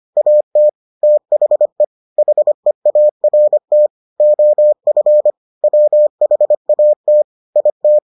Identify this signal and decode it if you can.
Morse code, heard as a keyed tone, about 25 wpm, 600 Hz. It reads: AT THE HEART OF WHAT IT